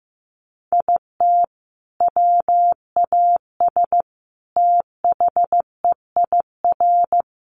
Morse code, heard as a keyed tone, about 15 wpm, 700 Hz. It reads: IT WAS THEIR
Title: Sentences from Top 1200 Words 15wpm